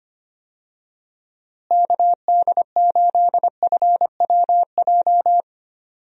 Morse code, heard as a keyed tone, about 25 wpm, 700 Hz. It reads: KD8FWJ
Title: USA Call Signs - 6 Characters - Repeated 3x - 25wpm